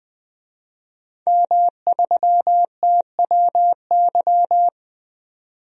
Morse code, heard as a keyed tone, about 20 wpm, 700 Hz. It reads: M3TWY
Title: DXCC-100 Call Signs - 5 Characters - 20wpm